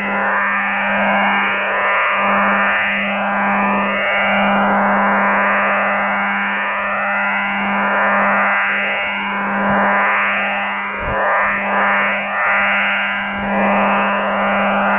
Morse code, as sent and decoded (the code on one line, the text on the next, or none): none